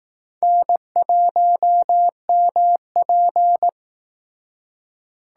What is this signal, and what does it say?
Morse code, heard as a keyed tone, about 19 wpm, 700 Hz. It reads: N1MP
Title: Speed Racing - USA Call Signs 15wpm